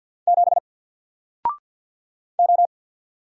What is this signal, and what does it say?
Morse code, heard as a keyed tone, about 50 wpm, 700 Hz. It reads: 7 O
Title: Single Letter-Number Alt 50wpm